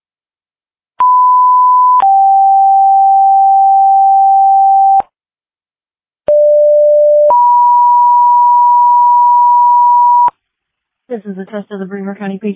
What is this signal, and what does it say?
Two-tone page, heard as a one-second tone, then a three-second tone, about 1000 then 780 Hz, about 590 then 970 Hz, with speech. It is heard from 1.0 to 10.3 s.